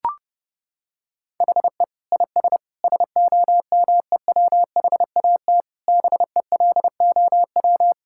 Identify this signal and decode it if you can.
Morse code, heard as a keyed tone, about 32 wpm, 700 Hz. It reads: HE IS SOMEWHAT BELOW
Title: Sentences from Top 1600 Words 30wpm